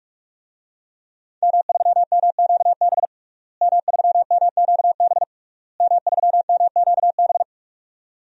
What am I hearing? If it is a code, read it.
Morse code, heard as a keyed tone, about 46 wpm, 700 Hz. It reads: M3MQB M3MQB M3MQB
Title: DXCC-100 Call Signs - 5 Characters - Repeated 3x - 45wpm